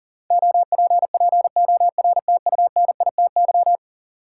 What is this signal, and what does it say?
Morse code, heard as a keyed tone, about 37 wpm, 700 Hz. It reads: OPPORTUNITY
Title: Sentences from Top 1000 Words - New Word Drill - 40wpm